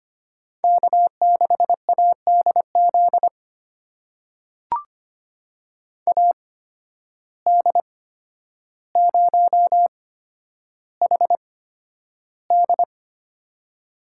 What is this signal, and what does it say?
Morse code, heard as a keyed tone, about 25 wpm, spaced as for 11 wpm, 700 Hz. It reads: K6ADZ AD0HD
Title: USA Call Signs - 5 Characters - 8x Character Spacing - 25wpm